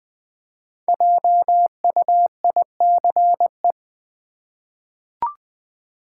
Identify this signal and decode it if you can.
Morse code, heard as a keyed tone, about 20 wpm, 700 Hz. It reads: JUICE
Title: Five Letter Words - 20wpm